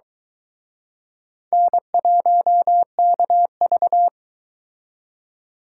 Morse code, heard as a keyed tone, about 23 wpm, 700 Hz. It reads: N1KV